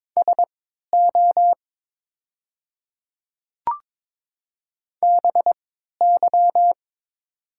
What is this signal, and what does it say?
Morse code, heard as a keyed tone, about 22 wpm, spaced as for 13 wpm, 700 Hz. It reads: SO BY
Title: 3x Character Spacing - Top 200 Words - 22wpm